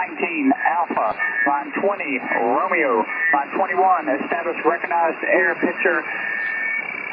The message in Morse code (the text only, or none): none